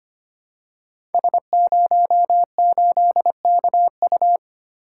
Morse code, heard as a keyed tone, about 25 wpm, 700 Hz. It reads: S08KU
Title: Code Groups - 5 Letter:Numbers - 25wpm